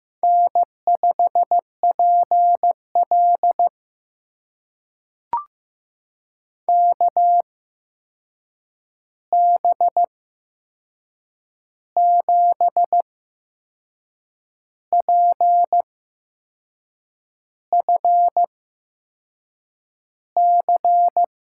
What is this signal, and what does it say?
Morse code, heard as a keyed tone, about 15 wpm, spaced as for 6 wpm, 700 Hz. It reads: N5PL KB7PFC
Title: USA Call Signs - 8x Character Spacing - 15wpm